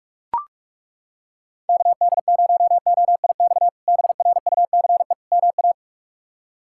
Morse code, heard as a keyed tone, about 41 wpm, 700 Hz. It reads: KD0OIX BRUCE MA
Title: K1USN SST QSO Practice - 45wpm